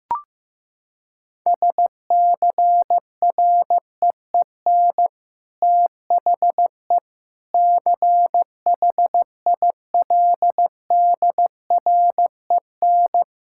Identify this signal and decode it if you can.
Morse code, heard as a keyed tone, about 15 wpm, 700 Hz. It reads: SCREEN THE CHILDREN
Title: Sentences from Top 1000 Words - Review Part 6 - 15wpm